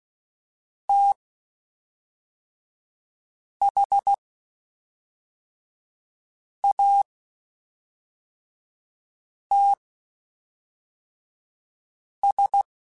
Morse code, heard as a keyed tone, about 16 wpm, spaced as for 3 wpm, 800 Hz. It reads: THATS
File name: QSOs / QSOs-mp3s / WD0FBT 1@5